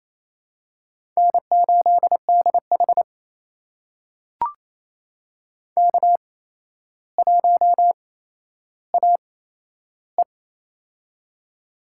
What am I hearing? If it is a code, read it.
Morse code, heard as a keyed tone, about 28 wpm, spaced as for 12 wpm, 700 Hz. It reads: N8DH K1AE
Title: USA Call Signs - 4 Characters - 8x Character Spacing - 28wpm